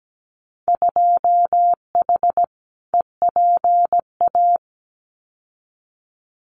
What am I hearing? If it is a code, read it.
Morse code, heard as a keyed tone, about 17 wpm, 700 Hz. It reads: 2H EPA
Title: Winter Field Day Exchanges Encore 17wpm